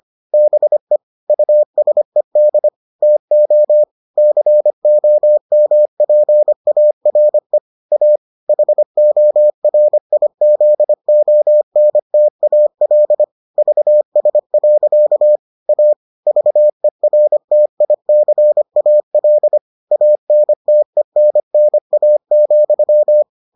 Morse code, heard as a keyed tone, about 25 wpm, 600 Hz. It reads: BE USED TO COMPARE A HORIZONTAL VS. A VERTICAL ANTENNA,